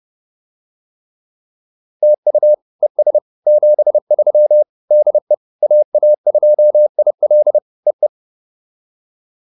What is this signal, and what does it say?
Morse code, heard as a keyed tone, about 30 wpm, 590 Hz. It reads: TU ES 73 DE AA2IL E E